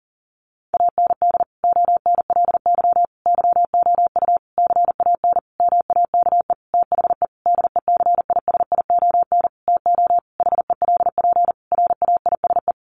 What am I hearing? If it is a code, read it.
Morse code, heard as a keyed tone, about 40 wpm, 700 Hz. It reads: AND ONLY YOU CAN MAKE THE DECISION TO HELP RAISE
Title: Sentences from Top 1400 Words 40wpm